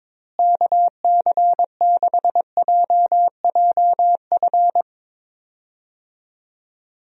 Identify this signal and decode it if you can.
Morse code, heard as a keyed tone, about 22 wpm, 700 Hz. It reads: KC6JJF
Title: USA Call Signs - 6 Characters - Repeated 2x - 22wpm